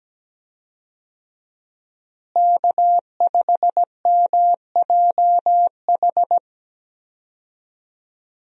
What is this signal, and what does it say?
Morse code, heard as a keyed tone, about 17 wpm, 700 Hz. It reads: K5MJH
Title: USA Call Signs - 5 Characters - Repeated 2x - 17wpm